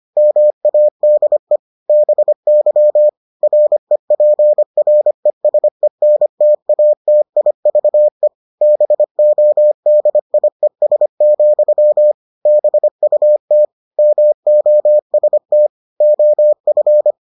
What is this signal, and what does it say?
Morse code, heard as a keyed tone, about 25 wpm, 600 Hz. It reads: MADE BY REPRESENTATIVE BODIES, BUT MOST OF